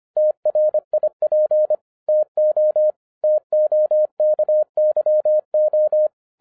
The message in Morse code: TRIP TO TOKYO

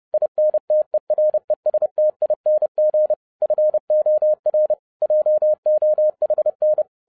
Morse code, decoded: INTERESTING FOR JOHN

